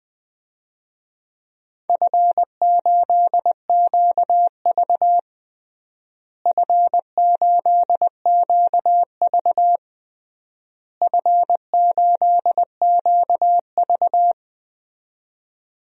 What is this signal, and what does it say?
Morse code, heard as a keyed tone, about 20 wpm, 700 Hz. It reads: F8QV F8QV F8QV
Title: DXCC-100 Call Signs - 4 Characters - Repeated 3x - 20wpm